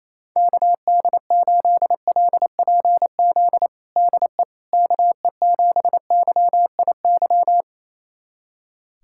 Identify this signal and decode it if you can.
Morse code, heard as a keyed tone, about 28 wpm, 700 Hz. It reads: KD8LPZ DE KE7YIY
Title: USA Callsign DE Callsign - 6 Characters - 28wpm